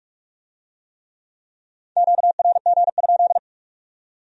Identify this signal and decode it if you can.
Morse code, heard as a keyed tone, about 46 wpm, 700 Hz. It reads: QRG?